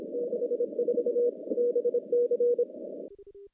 Morse code, heard as a keyed tone, about 27 wpm, 480 Hz. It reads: 64 BC
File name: contests / CW-SS-k5tr- -0400